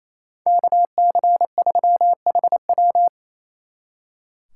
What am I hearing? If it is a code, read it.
Morse code, heard as a keyed tone, about 33 wpm, 700 Hz. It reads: KC3HW
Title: Speed Racing - USA 2x2 Call Signs 28wpm